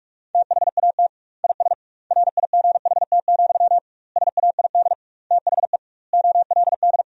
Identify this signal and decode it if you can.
Morse code, heard as a keyed tone, about 46 wpm, 700 Hz. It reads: THAT IS RIGHT, SAID THE OLD
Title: 2x Word-Spacing - Sentences from Top 200 Words - 45wpm